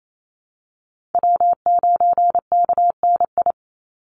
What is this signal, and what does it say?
Morse code, heard as a keyed tone, about 28 wpm, 700 Hz. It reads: W9KNI
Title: Call Signs Repeated 3x - 28wpm